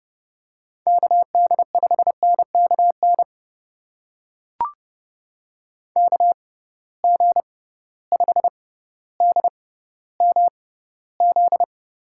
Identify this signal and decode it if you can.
Morse code, heard as a keyed tone, about 32 wpm, spaced as for 16 wpm, 700 Hz. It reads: KD5NKN KG5DMZ
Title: USA Call Signs - 6 Characters - 6x Character Spacing - 30wpm